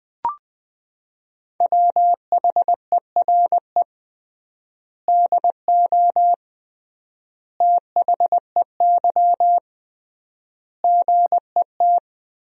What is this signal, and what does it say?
Morse code, heard as a keyed tone, about 20 wpm, 700 Hz. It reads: WHERE DO THEY GET
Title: 3x Word-Spacing - Sentences from Top 200 Words - 20wpm